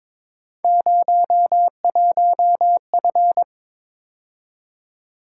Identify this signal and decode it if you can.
Morse code, heard as a keyed tone, about 23 wpm, 700 Hz. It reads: 01F
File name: practice / 22wpm / Three Letter-Number Combinations - 22wpm